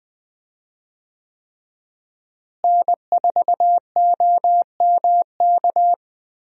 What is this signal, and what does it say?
Morse code, heard as a keyed tone, about 20 wpm, 700 Hz. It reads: N4OMK
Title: USA Call Signs - 5 Characters - Repeated 3x - 20wpm